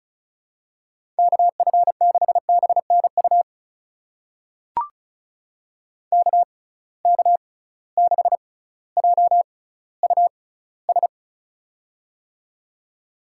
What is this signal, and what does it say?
Morse code, heard as a keyed tone, about 34 wpm, spaced as for 18 wpm, 700 Hz. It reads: KF6BNU KK6JUS
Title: USA Call Signs - 6 Characters - 6x Character Spacing - 35wpm